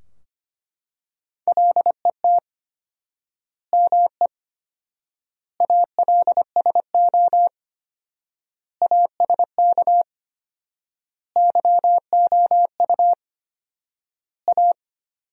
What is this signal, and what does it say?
Morse code, heard as a keyed tone, about 25 wpm, 700 Hz. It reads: LET ME ALSO ASK YOU A
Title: 4x Word-Spacing - Sentences from Top 400 Words - 25wpm